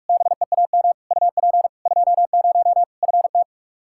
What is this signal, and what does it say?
Morse code, heard as a keyed tone, about 41 wpm, 700 Hz. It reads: BEAM UP 20 FT
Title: Practice QSO - Exchange 2 - Info - 45wpm